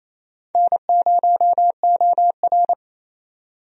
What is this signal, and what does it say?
Morse code, heard as a keyed tone, about 28 wpm, 700 Hz. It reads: N0OR
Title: USA Call Signs - 4 Characters - 28wpm